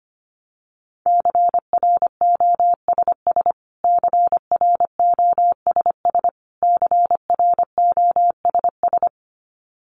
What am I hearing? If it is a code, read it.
Morse code, heard as a keyed tone, about 25 wpm, 700 Hz. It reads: CROSS CROSS CROSS
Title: Top 300 Words Repeated 25wpm